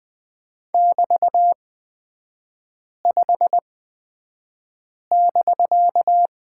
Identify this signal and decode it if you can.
Morse code, heard as a keyed tone, about 20 wpm, spaced as for 5 wpm, 700 Hz. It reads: <BT>5<BK>
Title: 250 Lesson 40e - BK - Three Characters 20wpm